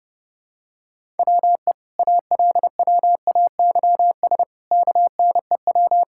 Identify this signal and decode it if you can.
Morse code, heard as a keyed tone, about 30 wpm, 700 Hz. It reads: WE ALWAYS KNEW
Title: Sentences from Top 1000 Words - Review Part 6 - 30wpm